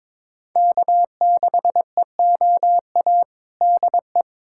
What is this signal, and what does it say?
Morse code, heard as a keyed tone, about 22 wpm, 700 Hz. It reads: K6EOA DE
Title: USA Callsign DE Callsign - 5 Characters - 22wpm